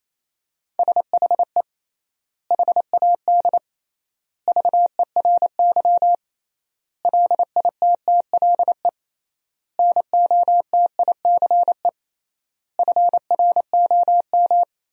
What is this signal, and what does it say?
Morse code, heard as a keyed tone, about 28 wpm, 700 Hz. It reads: SHE HAD VERY LITTLE NOTICE FROM
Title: 3x Word-Spacing - Sentences from Top 400 Words - 28wpm